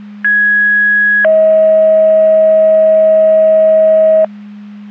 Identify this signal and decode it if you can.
Two-tone page, heard as a one-second tone, then a three-second tone, about 1650 then 640 Hz. It is heard from 0.2 to 4.2 s.
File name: Portage 154.385 Paging